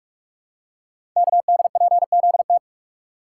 Morse code, heard as a keyed tone, about 44 wpm, 690 Hz. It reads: KDPZT
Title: Code Groups - 5 Letters - 45wpm